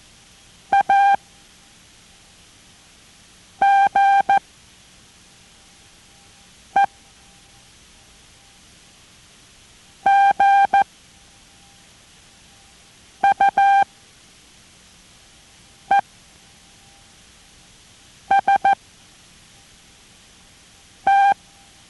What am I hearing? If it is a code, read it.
Morse code, heard as a keyed tone, about 14 wpm, spaced as for 3 wpm, 790 Hz. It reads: AGE GUEST